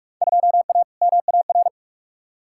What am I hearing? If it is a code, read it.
Morse code, heard as a keyed tone, about 41 wpm, 700 Hz. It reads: 2A MAR
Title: Field Day Exchanges - 45wpm